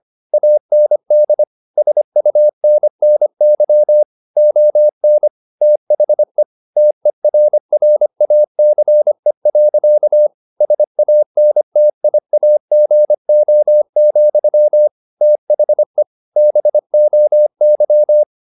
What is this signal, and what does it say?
Morse code, heard as a keyed tone, about 25 wpm, 600 Hz. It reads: AND SUNNY ON THE TERRACE. SANTIAGO, THE BOY
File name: the old man and the sea part 1 25wpm